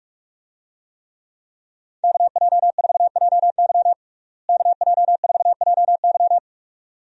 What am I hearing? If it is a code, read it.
Morse code, heard as a keyed tone, about 46 wpm, 700 Hz. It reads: KJ4JY KJ4JY
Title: USA Call Signs - 5 Characters - Repeated 2x - 45wpm